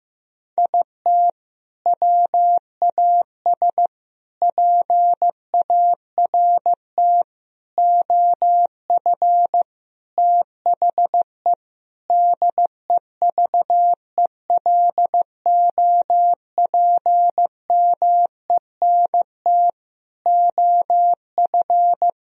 Morse code, heard as a keyed tone, about 16 wpm, 700 Hz. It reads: IT WAS PART OF THE DEVELOPMENT OF
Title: Sentences from Top 1000 Words - Review Part 4 - 15wpm